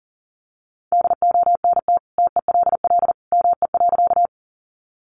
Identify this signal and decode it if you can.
Morse code, heard as a keyed tone, about 40 wpm, 700 Hz. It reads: DONT TELL ME.
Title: Sentences from Top 300 Words 40wpm